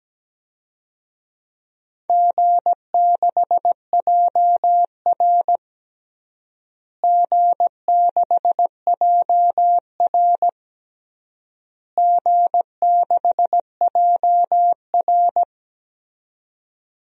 Morse code, heard as a keyed tone, about 17 wpm, 700 Hz. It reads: G6JR G6JR G6JR
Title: DXCC-100 Call Signs - 4 Characters - Repeated 3x - 17wpm